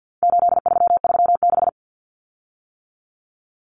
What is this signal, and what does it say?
Morse code, heard as a keyed tone, about 48 wpm, 700 Hz. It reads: 8336
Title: Common Cut Numbers - 4-Digit 50wpm